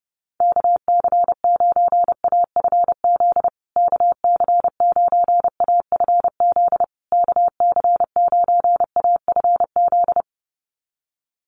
Morse code, heard as a keyed tone, about 30 wpm, 700 Hz. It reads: KC9AFZ KC9AFZ KC9AFZ
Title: USA Call Signs Repeated 3x 30wpm